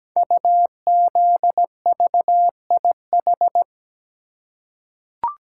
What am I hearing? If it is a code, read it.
Morse code, heard as a keyed tone, about 17 wpm, 700 Hz. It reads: UZVIH
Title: Code Groups - 5 Letters - 17wpm